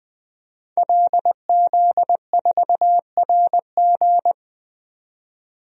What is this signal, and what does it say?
Morse code, heard as a keyed tone, about 20 wpm, 700 Hz. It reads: LZ4RG
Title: DXCC-100 Call Signs - 5 Characters - 20wpm